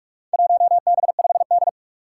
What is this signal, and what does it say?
Morse code, heard as a keyed tone, about 44 wpm, 700 Hz. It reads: 1B5D
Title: Four Challenging Letter-Number Combinations 45wpm